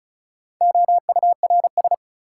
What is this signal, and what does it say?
Morse code, heard as a keyed tone, about 34 wpm, 700 Hz. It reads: OURS
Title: Sentences from Top 1000 Words - Review Part 5 - New Word Drill - 35wpm